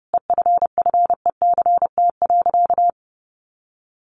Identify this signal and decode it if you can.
Morse code, heard as a keyed tone, about 30 wpm, 700 Hz. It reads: EFFECT.
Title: Sentences from Top 1300 Words 30wpm